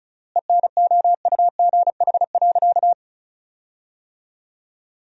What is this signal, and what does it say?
Morse code, heard as a keyed tone, about 36 wpm, 700 Hz. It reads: ENOUGH.
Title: 3x Word-Spacing - Sentences from Top 400 Words - 35wpm